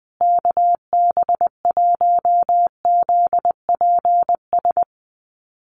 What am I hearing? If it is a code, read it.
Morse code, heard as a keyed tone, about 20 wpm, 700 Hz. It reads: KB1ZPS
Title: USA Call Signs Repeated 2x 20wpm